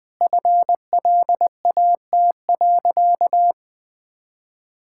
Morse code, heard as a keyed tone, about 21 wpm, 700 Hz. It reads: FLAT.
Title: Easy Sentences with Hints - 20wpm